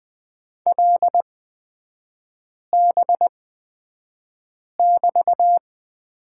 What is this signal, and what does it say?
Morse code, heard as a keyed tone, about 20 wpm, spaced as for 5 wpm, 700 Hz. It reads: LB<BT>
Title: 243 Lesson 39e - Letter z - Three Characters 20wpm